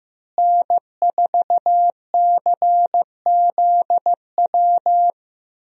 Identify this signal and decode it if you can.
Morse code, heard as a keyed tone, about 15 wpm, 700 Hz. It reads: N4CZW
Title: USA Call Signs - 5 Characters - Repeated 2x - 15wpm